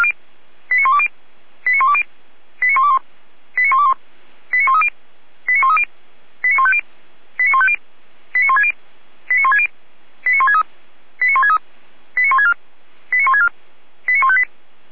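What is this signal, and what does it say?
Five-tone selective call (ZVEI), heard as quick runs of five tones, tones of about 80 ms. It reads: 89120 89120 89121 89121 89130 89130 89150 89150 89160 89160 89163 89163 89164 89164 89158
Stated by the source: ZVEI five-tone call received by SDR.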